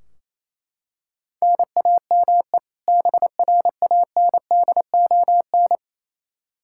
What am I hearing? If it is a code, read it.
Morse code, heard as a keyed tone, about 28 wpm, 700 Hz. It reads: NAME BRANDON
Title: K1USN SST QSO Practice 28wpm